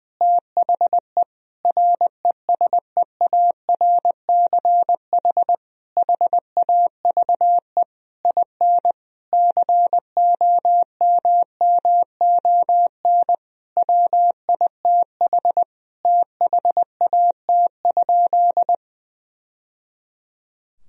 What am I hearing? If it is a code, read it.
Morse code, heard as a keyed tone, about 19 wpm, 700 Hz. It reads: THE RESEARCH HAVE IN COMMON WITH THAT?
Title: Sentences from Top 1000 Words - Review Part 4 - 20wpm